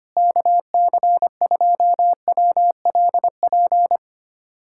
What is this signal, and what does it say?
Morse code, heard as a keyed tone, about 25 wpm, 700 Hz. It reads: KC2WLP
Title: USA Callsign DE Callsign 25wpm